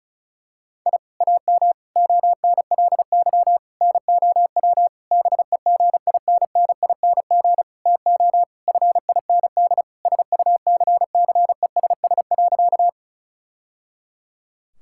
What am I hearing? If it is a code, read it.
Morse code, heard as a keyed tone, about 36 wpm, 700 Hz. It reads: I AM ONLY NOW BEGINNING TO FIND SUCCESS.
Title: Sentences from Top 1000 Words - Review Part 5 - 35wpm